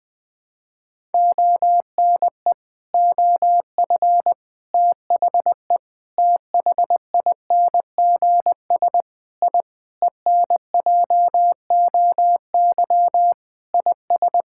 Morse code, heard as a keyed tone, about 20 wpm, 700 Hz. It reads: ONE OF THE THINGS I ENJOY IS
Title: Sentences with 5-Word Phrases - 20wpm